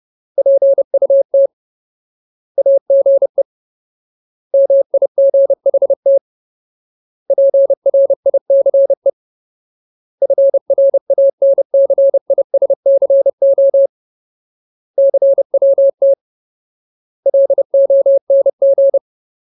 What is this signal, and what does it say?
Morse code, heard as a keyed tone, about 30 wpm, 550 Hz. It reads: PUT AGE MIGHT PRICE FRANCISCO CWT LONG
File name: Ultimate Word List / 30WPM Ultimate